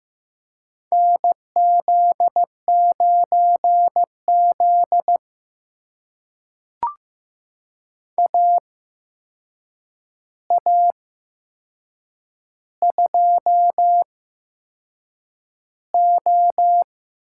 Morse code, heard as a keyed tone, about 15 wpm, spaced as for 6 wpm, 700 Hz. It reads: NZ9Z AA2O
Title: USA Call Signs - 8x Character Spacing - 15wpm